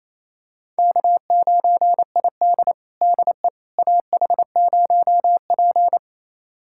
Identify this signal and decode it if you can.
Morse code, heard as a keyed tone, about 27 wpm, 700 Hz. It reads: K9ID DE AH0P
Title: USA Callsign DE Callsign - 4 Characters - 28wpm